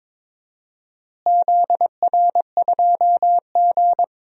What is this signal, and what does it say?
Morse code, heard as a keyed tone, about 21 wpm, 700 Hz. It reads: ZR2G
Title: DXCC-100 Call Signs - 4 Characters - Repeated 3x - 22wpm